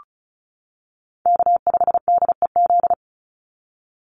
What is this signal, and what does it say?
Morse code, heard as a keyed tone, about 34 wpm, 700 Hz. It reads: K5DEZ
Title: NA SOTA Chaser Callsigns 35wpm